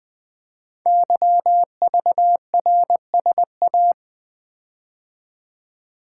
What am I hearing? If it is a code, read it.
Morse code, heard as a keyed tone, about 20 wpm, 700 Hz. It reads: YVRSA
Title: Code Groups - 5 Letters - 20wpm